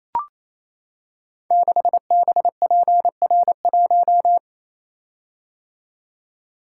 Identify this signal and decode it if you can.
Morse code, heard as a keyed tone, about 28 wpm, 700 Hz. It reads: 6BPR1
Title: Code Groups - 5 Letter:Numbers - 28wpm